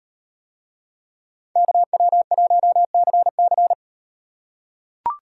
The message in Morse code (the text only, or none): KW1CC